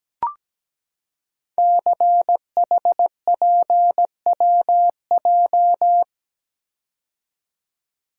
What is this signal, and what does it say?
Morse code, heard as a keyed tone, about 17 wpm, 700 Hz. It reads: CHPWJ
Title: Code Groups - 5 Letters - 17wpm